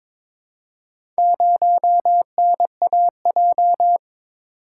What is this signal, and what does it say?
Morse code, heard as a keyed tone, about 22 wpm, 700 Hz. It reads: 0NAJ
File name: practice / 22wpm / Four Letter-Number Combinations - 22wpm